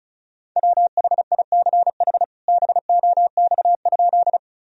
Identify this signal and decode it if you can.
Morse code, heard as a keyed tone, about 36 wpm, 700 Hz. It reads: WHICH BOX?
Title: Sentences from Top 1000 Words - Review Part 2 - 35wpm